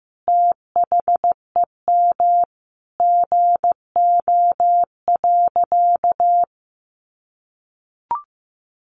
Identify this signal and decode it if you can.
Morse code, heard as a keyed tone, about 14 wpm, 700 Hz. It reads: THEM GO.
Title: Sentences from Top 800 Words 15wpm